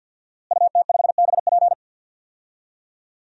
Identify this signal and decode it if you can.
Morse code, heard as a keyed tone, about 47 wpm, 700 Hz. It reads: UT5BP
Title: DXCC-100 Call Signs - 5 Characters - Repeated 3x - 50wpm